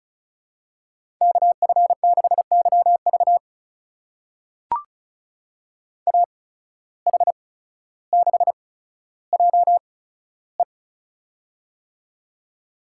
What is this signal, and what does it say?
Morse code, heard as a keyed tone, about 36 wpm, spaced as for 15 wpm, 700 Hz. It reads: KF6YV AH6JE
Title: USA Call Signs - 5 Characters - 8x Character Spacing - 35wpm